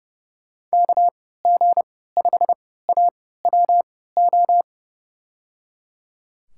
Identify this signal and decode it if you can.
Morse code, heard as a keyed tone, about 30 wpm, spaced as for 17 wpm, 700 Hz. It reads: KG5AWO
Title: USA Call Signs - 3x Character Spacing - 30wpm